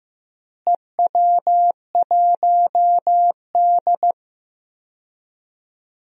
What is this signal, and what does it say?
Morse code, heard as a keyed tone, about 15 wpm, 700 Hz. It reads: EW1D
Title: DXCC-100 Call Signs - 4 Characters - Repeated 3x - 15wpm